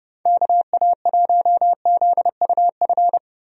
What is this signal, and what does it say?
Morse code, heard as a keyed tone, about 32 wpm, 700 Hz. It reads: KA1ZUF
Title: USA Call Signs - 6 Characters - 30wpm